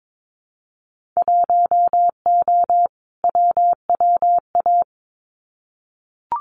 Morse code, heard as a keyed tone, about 22 wpm, 700 Hz. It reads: 1O WWA
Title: Winter Field Day Exchanges 22wpm